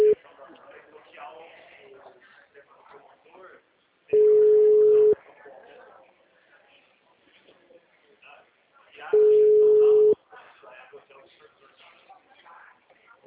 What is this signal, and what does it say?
Ringback tone (Europe, 425 Hz), with speech.
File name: out- -2019- - - .48307